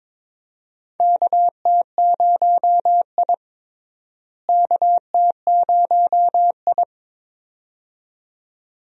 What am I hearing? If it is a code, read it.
Morse code, heard as a keyed tone, about 22 wpm, 700 Hz. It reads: KT0I KT0I
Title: USA Call Signs - 4 Characters - Repeated 2x - 22wpm